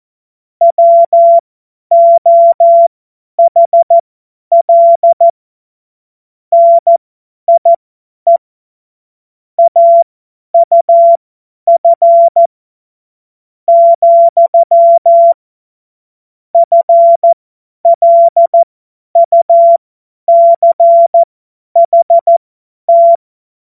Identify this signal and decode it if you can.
Morse code, heard as a keyed tone, about 14 wpm, spaced as for 9 wpm, 680 Hz. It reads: WOHL NIE AUF, FLUCHT
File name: Morsekurs / Witze / Witz 13. Juli 2021, 50 F70